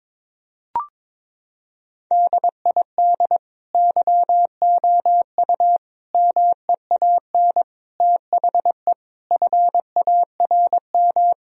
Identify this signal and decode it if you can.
Morse code, heard as a keyed tone, about 21 wpm, 700 Hz. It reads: DID YOU MEAN THE FARM
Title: Sentences from Top 1000 Words - Review Part 2 - 22wpm